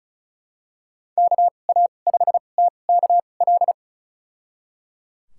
Morse code, heard as a keyed tone, about 34 wpm, spaced as for 25 wpm, 700 Hz. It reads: KA5TKL
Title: USA Call Signs - 6 Characters - 2x Character Spacing - 35wpm